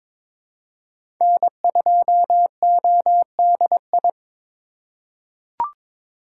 Morse code, heard as a keyed tone, about 22 wpm, 700 Hz. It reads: N2ODI
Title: USA Call Signs - 5 Characters - 3x Character Spacing - 22wpm